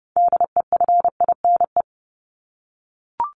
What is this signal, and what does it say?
Morse code, heard as a keyed tone, about 30 wpm, 700 Hz. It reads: DEFINE
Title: ED Suffix - Top 100 Words with Base Words 30wpm